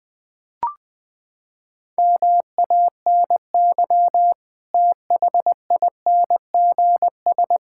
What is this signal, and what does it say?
Morse code, heard as a keyed tone, about 20 wpm, 700 Hz. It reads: MANY THINGS
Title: Sentences with 7-Word Phrases - 20wpm